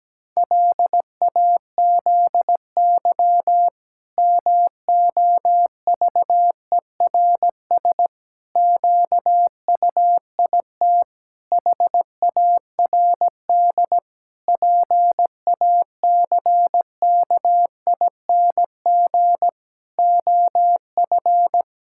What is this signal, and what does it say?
Morse code, heard as a keyed tone, about 17 wpm, 700 Hz. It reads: LAZY MOVERS QUIT HARD PACKING OF
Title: Pangram Challenge v2 - 17wpm